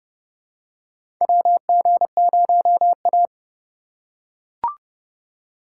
Morse code, heard as a keyed tone, about 30 wpm, 700 Hz. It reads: WG0A
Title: USA Call Signs - 4 Characters - 2x Character Spacing - 30wpm